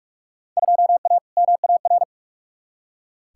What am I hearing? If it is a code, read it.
Morse code, heard as a keyed tone, about 46 wpm, 700 Hz. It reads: 2A MAR
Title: Field Day Exchanges - 45wpm